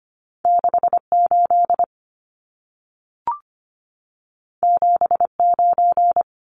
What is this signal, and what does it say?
Morse code, heard as a keyed tone, about 25 wpm, 700 Hz. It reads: 68 79
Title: Numbers - 2-Digit 25wpm